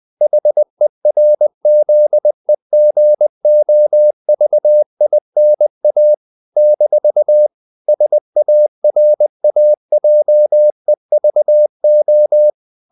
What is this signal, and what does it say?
Morse code, heard as a keyed tone, about 20 wpm, 600 Hz. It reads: HERZEGOVINA - SARAJEVO
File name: CW / Practice / NationalCapitals20